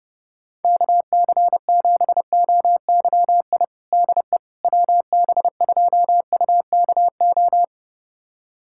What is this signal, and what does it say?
Morse code, heard as a keyed tone, about 31 wpm, 700 Hz. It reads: KC7OYI DE WB2UKO
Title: USA Callsign DE Callsign - 6 Characters - 30wpm